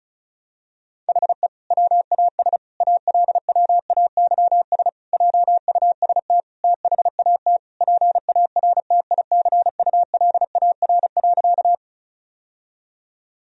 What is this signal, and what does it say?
Morse code, heard as a keyed tone, about 34 wpm, 700 Hz. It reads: HE WAS ALWAYS JUST THAT PARTICULAR.
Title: Sentences from Top 1000 Words - Review Part 4 - 35wpm